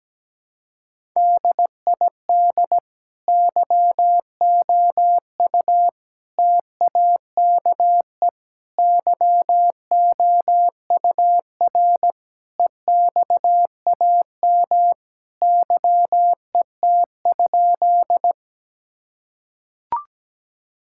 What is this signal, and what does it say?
Morse code, heard as a keyed tone, about 17 wpm, 700 Hz. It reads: DID YOU TAKE YOUR EXAM YET?
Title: Sentences from Top 1000 Words - Review Part 5 - 17wpm